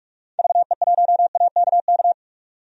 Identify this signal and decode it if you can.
Morse code, heard as a keyed tone, about 48 wpm, 700 Hz. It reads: VE1AKK
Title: DXCC-100 Call Signs - 6 Characters - 45wpm